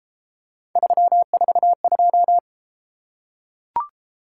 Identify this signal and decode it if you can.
Morse code, heard as a keyed tone, about 32 wpm, 700 Hz. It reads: 342